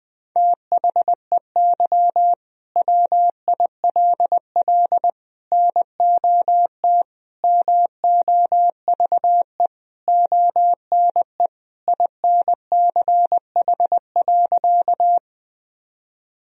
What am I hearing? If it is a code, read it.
Morse code, heard as a keyed tone, about 20 wpm, 700 Hz. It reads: THEY WILL NOT MOVE ONE INCH.
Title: Sentences from Top 1000 Words - Review Part 3 - 20wpm